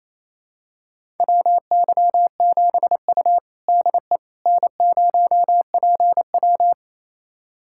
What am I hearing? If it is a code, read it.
Morse code, heard as a keyed tone, about 28 wpm, 700 Hz. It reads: WY7U DE N0PW
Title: USA Callsign DE Callsign - 4 Characters - 28wpm